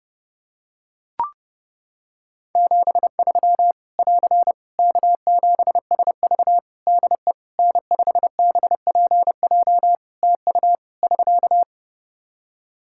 Morse code, heard as a keyed tone, about 30 wpm, 700 Hz. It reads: 73 <AR> K7SV DE N5BPJ TU <SK>
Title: Practice QSO - Exchange 4 - Ending - 30wpm